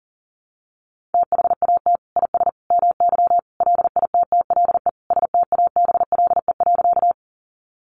Morse code, heard as a keyed tone, about 40 wpm, 700 Hz. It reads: THAT IS MY LITTLE STABLE.
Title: Sentences from Top 1500 Words 40wpm